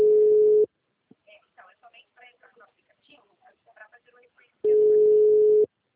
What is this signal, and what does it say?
Ringback tone (Europe, 425 Hz), with speech.
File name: out- -2016- - - .52029